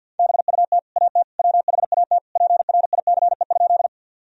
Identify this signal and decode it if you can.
Morse code, heard as a keyed tone, about 50 wpm, 700 Hz. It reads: BUT AT WHAT PRICE?
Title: Sentences from Top 1000 Words - Review Part 4 - 50wpm